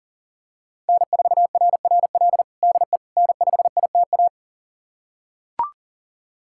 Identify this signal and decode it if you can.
Morse code, heard as a keyed tone, about 40 wpm, 700 Hz. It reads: N4RRL DE N5ITA
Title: USA Callsign DE Callsign - 5 Characters - 40wpm